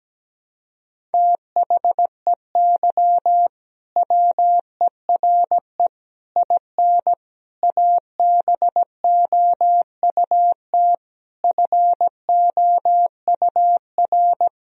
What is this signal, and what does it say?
Morse code, heard as a keyed tone, about 17 wpm, 700 Hz. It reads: THEY WERE IN ABOUT FOUR
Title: Sentences from Top 1000 Words - Review Part 1 - 17wpm